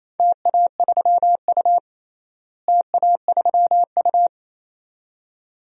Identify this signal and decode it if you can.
Morse code, heard as a keyed tone, about 28 wpm, 700 Hz. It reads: TA3U TA3U
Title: DXCC-100 Call Signs - 4 Characters - Repeated 2x - 28wpm